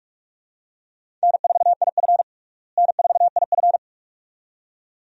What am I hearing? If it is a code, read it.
Morse code, heard as a keyed tone, about 46 wpm, 700 Hz. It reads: N4IF N4IF
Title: USA Call Signs - 4 Characters - Repeated 2x - 45wpm